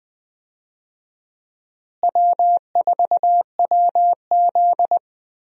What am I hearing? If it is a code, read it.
Morse code, heard as a keyed tone, about 20 wpm, 700 Hz. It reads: W4WZ
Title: USA Call Signs - 4 Characters - Repeated 3x - 20wpm